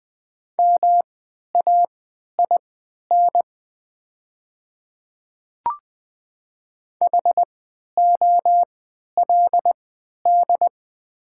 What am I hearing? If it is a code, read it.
Morse code, heard as a keyed tone, about 20 wpm, spaced as for 11 wpm, 690 Hz. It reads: MAIN HOLD
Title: 3x Character Spacing - Top 500 Words - 20wpm